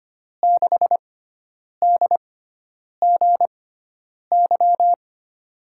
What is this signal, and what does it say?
Morse code, heard as a keyed tone, about 25 wpm, spaced as for 9 wpm, 700 Hz. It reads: 6DGY
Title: USA Call Signs - 5 Characters - 6x Character Spacing - 25wpm